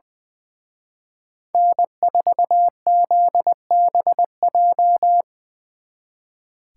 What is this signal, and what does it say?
Morse code, heard as a keyed tone, about 22 wpm, 700 Hz. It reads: N4ZBJ
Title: Speed Racing - USA 1x3 Call Signs 20wpm